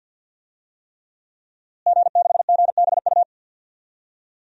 Morse code, heard as a keyed tone, about 46 wpm, 700 Hz. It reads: G6GBU
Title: DXCC-100 Call Signs - 5 Characters - Repeated 3x - 50wpm